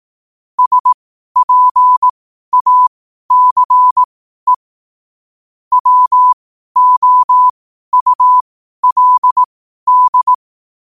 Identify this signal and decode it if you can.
Morse code, heard as a keyed tone, about 18 wpm, spaced as for 12 wpm, 1000 Hz. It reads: SPACE WOULD
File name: qotdfiles / 12wpm